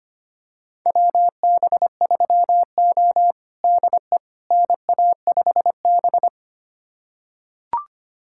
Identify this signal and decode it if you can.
Morse code, heard as a keyed tone, about 25 wpm, 700 Hz. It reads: WB3O DE NA5B
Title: USA Callsign DE Callsign - 4 Characters - 25wpm